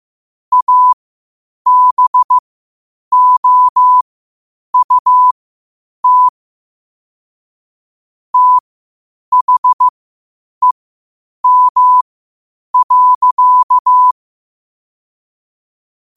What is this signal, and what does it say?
Morse code, heard as a keyed tone, about 15 wpm, spaced as for 8 wpm, 1000 Hz. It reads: ABOUT THEM.
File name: qotdfiles / 8wpm